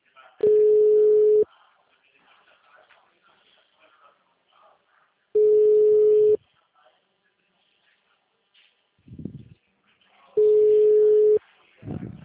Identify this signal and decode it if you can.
Ringback tone (Europe, 425 Hz), with speech.